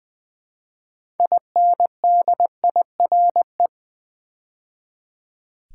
Morse code, heard as a keyed tone, about 20 wpm, 700 Hz. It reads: INDIRE
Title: Build A Word - 3 Letters At A Time - 8-Letter Words - 20wpm